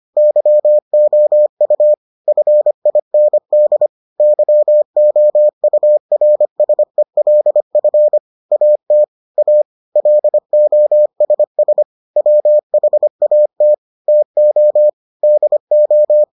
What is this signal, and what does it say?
Morse code, heard as a keyed tone, about 25 wpm, 600 Hz. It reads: YOU FIND YOURSELF AT A LOSS WHAT TO DO